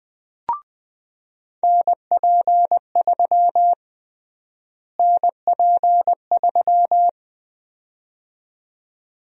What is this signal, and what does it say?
Morse code, heard as a keyed tone, about 21 wpm, 700 Hz. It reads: NP3 NP3
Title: USA Call Sign Prefixes - Repeated 2x - 20wpm